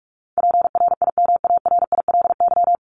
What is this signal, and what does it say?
Morse code, heard as a keyed tone, about 44 wpm, 700 Hz. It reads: PRIMARILY
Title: LY Suffix - Top 100 Words with Base Words 45wpm